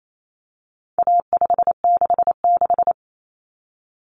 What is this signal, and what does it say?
Morse code, heard as a keyed tone, about 28 wpm, 700 Hz. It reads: A566
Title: Common Cut Numbers - 4-Digit 28wpm